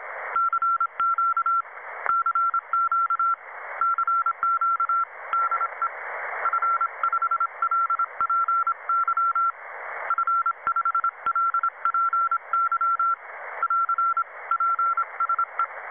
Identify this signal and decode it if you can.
Morse code, heard as a keyed tone, about 26 wpm, 1340 Hz. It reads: CQ CQ CQ DE F5LPY F5LPY <AR> PSE
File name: Websdr Recording 2014 07 21t07 22 13z 7011 4khz